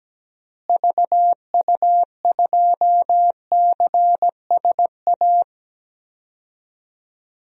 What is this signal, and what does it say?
Morse code, heard as a keyed tone, about 17 wpm, 700 Hz. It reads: VU2CSA
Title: DXCC-100 Call Signs - 6 Characters - Repeated 2x - 17wpm